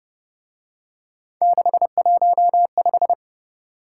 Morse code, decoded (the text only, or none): 615